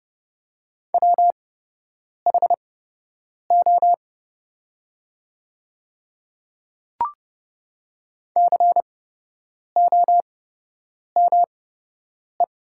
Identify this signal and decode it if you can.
Morse code, heard as a keyed tone, about 29 wpm, spaced as for 8 wpm, 700 Hz. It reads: WHO COME
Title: 8x Character Spacing - Top 100 Words - 30wpm